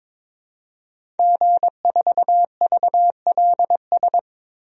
Morse code, heard as a keyed tone, about 22 wpm, 700 Hz. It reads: G4VLS
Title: DXCC-100 Call Signs - 5 Characters - Repeated 2x - 22wpm